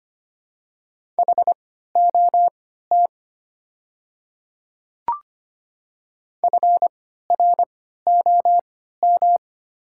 Morse code, heard as a keyed tone, about 25 wpm, spaced as for 14 wpm, 690 Hz. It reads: HOT FROM
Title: 3x Character Spacing - Top 100 Words - 25wpm